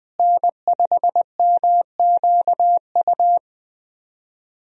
Morse code, heard as a keyed tone, about 20 wpm, 700 Hz. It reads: N5MQU
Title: USA Call Signs - 5 Characters - Repeated 2x - 20wpm